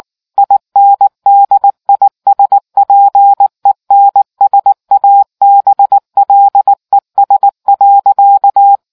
Morse code, heard as a keyed tone, about 19 wpm, 800 Hz. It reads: INDISPENSABLES.